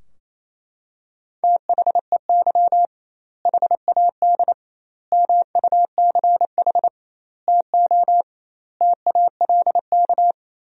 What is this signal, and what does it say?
Morse code, heard as a keyed tone, about 28 wpm, 700 Hz. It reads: THEY HAD MUCH TO TALK
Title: 2x Word-Spacing - Sentences from Top 400 Words - 28wpm